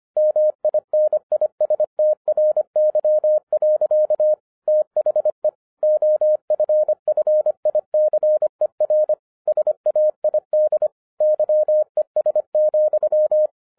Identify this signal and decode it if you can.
Morse code, heard as a keyed tone, about 25 wpm, 610 Hz. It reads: MINISTRY. THE OFFICER SAID YES,